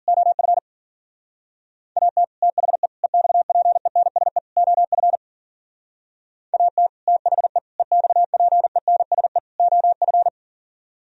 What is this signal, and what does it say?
Morse code, heard as a keyed tone, about 46 wpm, 700 Hz. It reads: OF AT THE EXPENSE OF AT THE EXPENSE OF
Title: Speed-Racing - Sets of 4 Words 40wpm